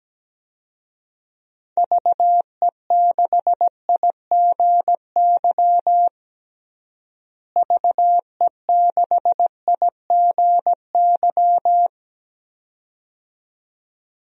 Morse code, heard as a keyed tone, about 17 wpm, 700 Hz. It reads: VE6IGY VE6IGY
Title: DXCC-100 Call Signs - 6 Characters - Repeated 2x - 17wpm